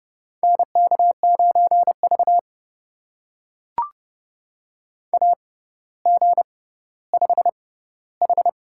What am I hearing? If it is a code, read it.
Morse code, heard as a keyed tone, about 32 wpm, spaced as for 16 wpm, 700 Hz. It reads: NK9V AG5H
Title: USA Call Signs - 4 Characters - 6x Character Spacing - 30wpm